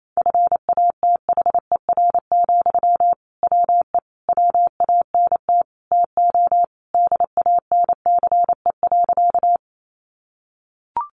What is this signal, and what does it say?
Morse code, heard as a keyed tone, about 28 wpm, 700 Hz. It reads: FATHER, WE WANT TO DANCE.
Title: Sentences from Top 700 Words 28wpm